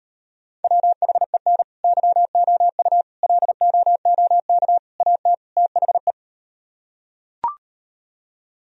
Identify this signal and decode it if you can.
Morse code, heard as a keyed tone, about 38 wpm, 700 Hz. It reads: WHEN YOU LOOK AT THE